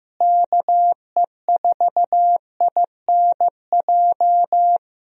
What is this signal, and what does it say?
Morse code, heard as a keyed tone, about 15 wpm, 700 Hz. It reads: KE4INJ
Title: USA Call Signs - 6 Characters - Repeated 3x - 15wpm